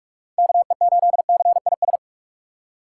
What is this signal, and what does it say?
Morse code, heard as a keyed tone, about 47 wpm, 700 Hz. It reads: KE8CIS
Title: USA Call Signs - 6 Characters - 6x Character Spacing - 45wpm